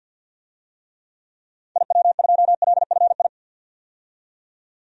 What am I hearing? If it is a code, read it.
Morse code, heard as a keyed tone, about 49 wpm, 700 Hz. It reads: IW2LFI
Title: DXCC-100 Call Signs - 6 Characters - 50wpm